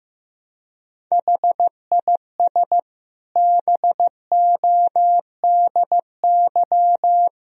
Morse code, heard as a keyed tone, about 15 wpm, 700 Hz. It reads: HIS BODY
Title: Sentences from Top 1000 Words - Review Part 2 - 15wpm